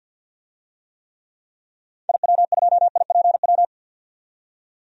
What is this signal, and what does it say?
Morse code, heard as a keyed tone, about 51 wpm, 700 Hz. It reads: IW2IPW
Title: DXCC-100 Call Signs - 6 Characters - 50wpm